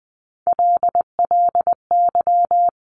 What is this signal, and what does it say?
Morse code, heard as a keyed tone, about 20 wpm, 700 Hz. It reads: LLY
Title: Top Three Letter Combinations in Words 20wpm